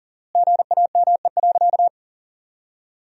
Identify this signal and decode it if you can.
Morse code, heard as a keyed tone, about 37 wpm, 700 Hz. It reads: GAME.
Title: 2x Word-Spacing - Sentences from Top 500 Words - 40wpm